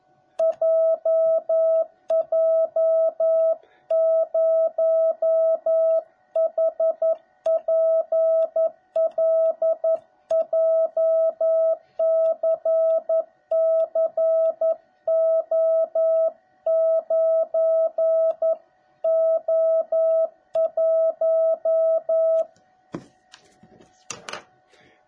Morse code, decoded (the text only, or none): JJ0HPL JCCO9O1